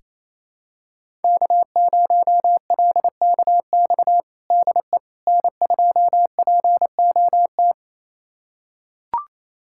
Morse code, heard as a keyed tone, about 28 wpm, 700 Hz. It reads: K0LKX DE N2POT
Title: USA Callsign DE Callsign - 5 Characters - 28wpm